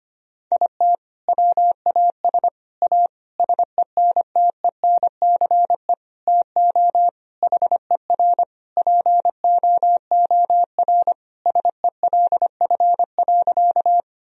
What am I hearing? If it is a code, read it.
Morse code, heard as a keyed tone, about 25 wpm, 700 Hz. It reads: IT WAS A SENTENCE TO HER POOR SELF.
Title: Sentences from Top 1000 Words - Review Part 1 - 25wpm